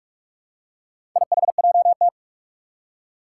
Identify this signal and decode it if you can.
Morse code, heard as a keyed tone, about 44 wpm, 700 Hz. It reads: IHJT
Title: Four Challenging Letter-Number Combinations 45wpm